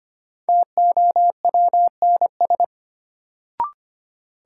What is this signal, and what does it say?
Morse code, heard as a keyed tone, about 25 wpm, 700 Hz. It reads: TOWNS
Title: Sentences from Top 1000 Words - Review Part 6 - New Word Drill - 25wpm